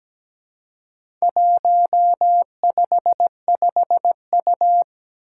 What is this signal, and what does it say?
Morse code, heard as a keyed tone, about 17 wpm, 700 Hz. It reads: 155U
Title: Four Challenging Letter-Number Combinations 17wpm